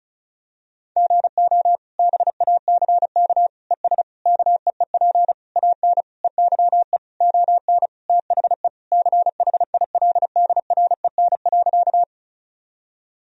Morse code, heard as a keyed tone, about 34 wpm, 700 Hz. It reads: GO BACK ES KEEP AN EYE ON THE CHILDREN.
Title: Sentences from Top 300 Words and Abbreviations 35wpm